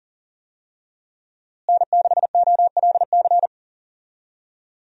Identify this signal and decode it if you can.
Morse code, heard as a keyed tone, about 40 wpm, 700 Hz. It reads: N6OLC
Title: USA Call Signs - 5 Characters - Repeated 2x - 40wpm